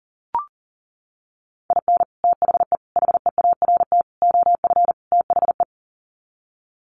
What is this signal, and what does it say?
Morse code, heard as a keyed tone, about 40 wpm, 700 Hz. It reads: IN THE HEART OF THE
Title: Sets of 5 Words 40wpm